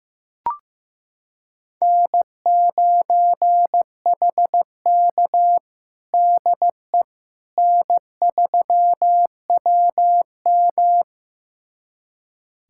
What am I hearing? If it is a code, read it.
Morse code, heard as a keyed tone, about 16 wpm, 700 Hz. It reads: N9HK DE N3WM
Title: USA Callsign DE Callsign - 4 Characters - 15wpm